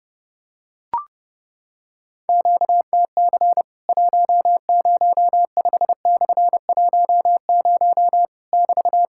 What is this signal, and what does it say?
Morse code, heard as a keyed tone, about 30 wpm, 700 Hz. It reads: QTC 105/10 <BT>
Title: Worked All Europe DX Contest - QTC Exchange - 30wpm